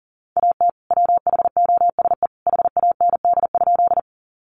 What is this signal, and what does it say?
Morse code, heard as a keyed tone, about 40 wpm, 700 Hz. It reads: AT WHOSE HAND?
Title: Sentences from Top 1200 Words 40wpm